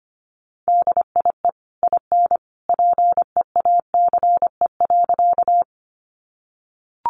Morse code, heard as a keyed tone, about 25 wpm, 700 Hz. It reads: DIE IN PEACE.
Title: Sentences from Top 1500 Words 25wpm